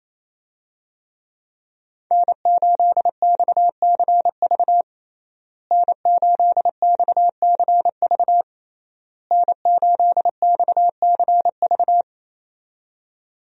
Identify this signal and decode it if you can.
Morse code, heard as a keyed tone, about 28 wpm, 700 Hz. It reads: N8XCV N8XCV N8XCV
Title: USA Call Signs - 5 Characters - Repeated 3x - 28wpm